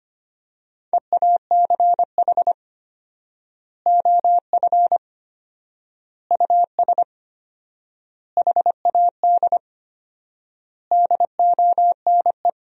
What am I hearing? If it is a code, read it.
Morse code, heard as a keyed tone, about 25 wpm, 700 Hz. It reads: EACH OF US HAD DONE
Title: 4x Word-Spacing - Sentences from Top 500 Words - 25wpm